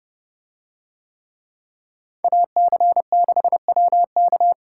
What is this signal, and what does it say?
Morse code, heard as a keyed tone, about 29 wpm, 700 Hz. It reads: AC6WK